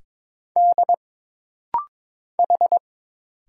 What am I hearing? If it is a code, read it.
Morse code, heard as a keyed tone, about 22 wpm, 700 Hz. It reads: D H
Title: Single Letters - Rapid-Fire 22wpm